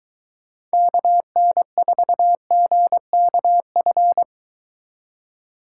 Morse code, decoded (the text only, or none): KN4GKF